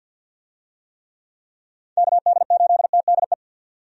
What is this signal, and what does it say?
Morse code, heard as a keyed tone, about 49 wpm, 700 Hz. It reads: KD8TDE
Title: USA Call Signs - 6 Characters - 50wpm